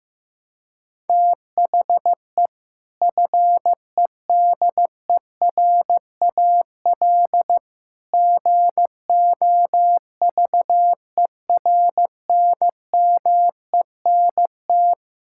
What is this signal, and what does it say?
Morse code, heard as a keyed tone, about 15 wpm, 700 Hz. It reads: THE FEDERAL GOVERNMENT
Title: Sentences from Top 1000 Words - Review Part 5 - 15wpm